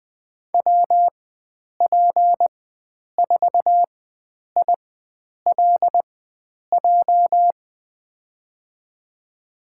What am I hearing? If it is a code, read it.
Morse code, heard as a keyed tone, about 20 wpm, spaced as for 9 wpm, 700 Hz. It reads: WP4ILJ
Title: USA Call Signs - 4x Character Spacing - 20wpm